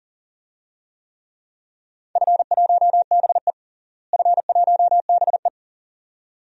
Morse code, heard as a keyed tone, about 40 wpm, 700 Hz. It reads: F1BE F1BE
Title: DXCC-100 Call Signs - 4 Characters - Repeated 2x - 40wpm